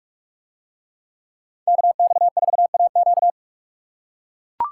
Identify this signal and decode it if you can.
Morse code, heard as a keyed tone, about 44 wpm, 700 Hz. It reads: KX4AQ